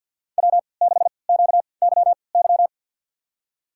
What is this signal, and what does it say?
Morse code, heard as a keyed tone, about 52 wpm, spaced as for 29 wpm, 700 Hz. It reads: W6QYY
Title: USA Call Signs - 5 Characters - 3x Character Spacing - 50wpm